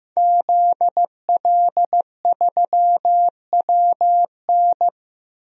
Morse code, heard as a keyed tone, about 15 wpm, 700 Hz. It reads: ZL3WN
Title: DXCC-100 Call Signs - 5 Characters - Repeated 3x - 15wpm